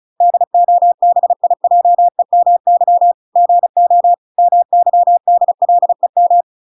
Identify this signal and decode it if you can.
Morse code, heard as a keyed tone, about 34 wpm, 700 Hz. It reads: DOBIJEMY GO MYDLEM